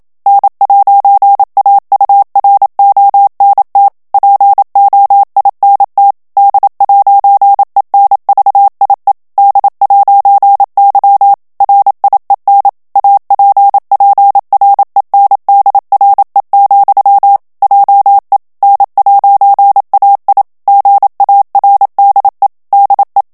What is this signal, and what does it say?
Morse code, heard as a keyed tone, about 27 wpm, 800 Hz. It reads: N'AURONT POINT D'ENVIE D'Y RIEN APPRENDRE, JE N'AI GARDE DE
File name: FRANCAIS TEXTE CLAIR 2 VITESSE 26 MOTS MINUTES